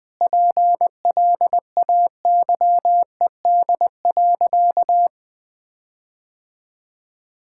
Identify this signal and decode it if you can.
Morse code, heard as a keyed tone, about 20 wpm, 700 Hz. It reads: PLAYED.
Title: Sentences with 6-Word Phrases - 20wpm